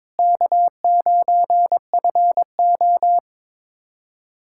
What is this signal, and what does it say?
Morse code, heard as a keyed tone, about 22 wpm, 700 Hz. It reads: K9FO
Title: USA Callsign DE Callsign - 4 Characters - 22wpm